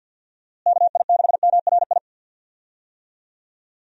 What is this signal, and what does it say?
Morse code, heard as a keyed tone, about 52 wpm, 700 Hz. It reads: KI6MRI
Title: USA Call Signs - 6 Characters - Repeated 3x - 50wpm